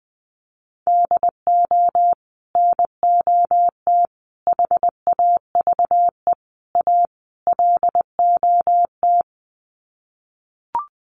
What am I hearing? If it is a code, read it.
Morse code, heard as a keyed tone, about 20 wpm, 700 Hz. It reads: DO NOT HAVE A LOT
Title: Sets of 5 Words Encore 20wpm